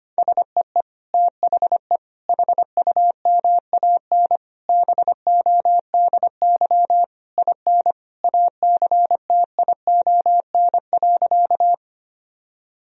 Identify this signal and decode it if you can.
Morse code, heard as a keyed tone, about 25 wpm, 700 Hz. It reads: SEE THE HUMAN BODY IN ACTION.
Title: Sentences from Top 1000 Words - Review Part 4 - 25wpm